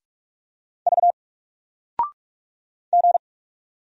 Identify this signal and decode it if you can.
Morse code, heard as a keyed tone, about 44 wpm, 700 Hz. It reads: V G
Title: Single Letter-Number Alt 45wpm